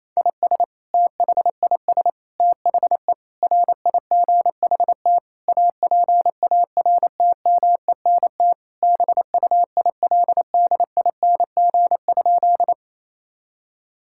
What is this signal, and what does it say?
Morse code, heard as a keyed tone, about 28 wpm, 700 Hz. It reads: IS THIS THE RIGHT APARTMENT BUILDING?
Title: Sentences from Top 1000 Words - Review Part 6 - 28wpm